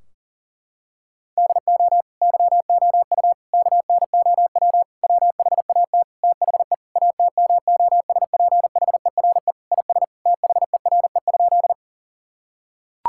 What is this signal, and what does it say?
Morse code, heard as a keyed tone, about 40 wpm, 700 Hz. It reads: DO YOU KNOW WHAT THE ATMOSPHERE IS THERE?
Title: Sentences from Top 1600 Words 40wpm